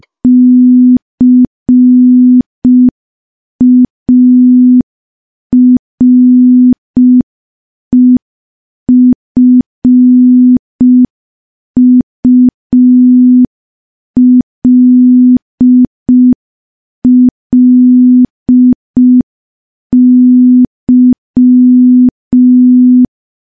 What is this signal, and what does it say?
Morse code, heard as a keyed tone, about 5 wpm, 250 Hz. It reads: CAREFULLY
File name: morse / morse-training-05WPM